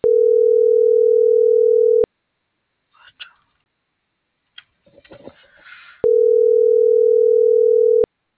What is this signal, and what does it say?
Ringback tone (North America), 440 + 480 Hz.